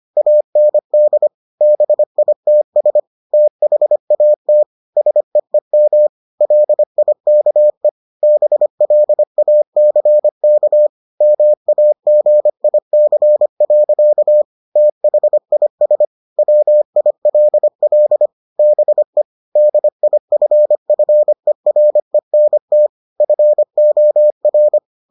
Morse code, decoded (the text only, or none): AND BITS THAT SEEM LIKE BLACK MAGIC. THIS WILL BE DIFFERENT FOR